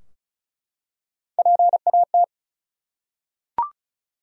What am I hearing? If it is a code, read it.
Morse code, heard as a keyed tone, about 34 wpm, 700 Hz. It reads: PAT
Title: USA Common Names - Max 5 Letters - 35wpm